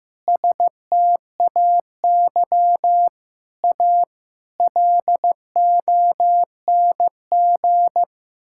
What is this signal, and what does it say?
Morse code, heard as a keyed tone, about 15 wpm, 700 Hz. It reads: STAY A LONG
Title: Sentences from Top 1000 Words - Review Part 6 - 15wpm